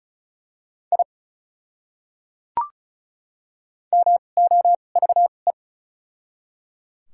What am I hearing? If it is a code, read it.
Morse code, heard as a keyed tone, about 34 wpm, spaced as for 25 wpm, 700 Hz. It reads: I MOVE
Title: 2x Character Spacing - Top 300 Words - 35wpm